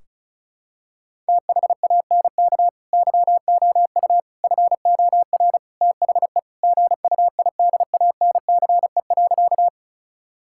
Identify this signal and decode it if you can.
Morse code, heard as a keyed tone, about 36 wpm, 700 Hz. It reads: THANK YOU FOR THE GUIDANCE.
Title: Sentences from Top 1600 Words 35wpm